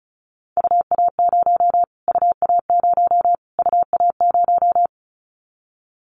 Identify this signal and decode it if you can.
Morse code, heard as a keyed tone, about 34 wpm, 700 Hz. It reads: UA0 UA0 UA0
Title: Callsign Prefixes for Top 25 DXCC - Repeated 3x - 35wpm